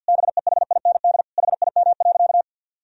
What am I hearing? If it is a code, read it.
Morse code, heard as a keyed tone, about 46 wpm, 700 Hz. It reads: BEHIND HIM.
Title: Sentences from Top 1000 Words - Review Part 1 - 50wpm